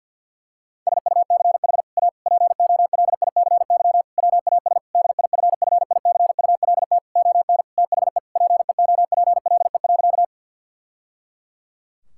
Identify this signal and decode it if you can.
Morse code, heard as a keyed tone, about 50 wpm, 700 Hz. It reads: SUCH A POLICY WAS DIFFICULT ON THE PEOPLE.
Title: Sentences from Top 1000 Words - Review Part 5 - 50wpm